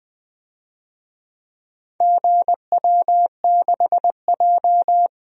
Morse code, heard as a keyed tone, about 21 wpm, 700 Hz. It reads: GW6J
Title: DXCC-100 Call Signs - 4 Characters - 20wpm